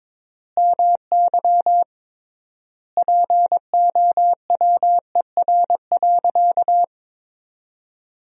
Morse code, heard as a keyed tone, about 22 wpm, 700 Hz. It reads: MY POWER.
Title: 3x Word-Spacing - Sentences from Top 500 Words - 22wpm